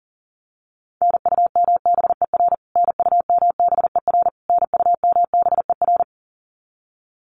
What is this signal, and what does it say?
Morse code, heard as a keyed tone, about 40 wpm, 700 Hz. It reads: NUMBER NUMBER NUMBER
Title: Top 200 Words Repeated 40wpm